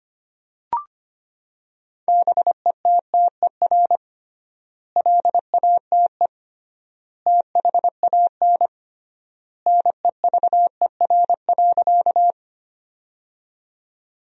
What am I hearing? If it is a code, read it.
Morse code, heard as a keyed tone, about 25 wpm, 700 Hz. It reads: BETTER LATE THAN NEVER.
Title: 3x Word-Spacing - Sentences from Top 400 Words - 25wpm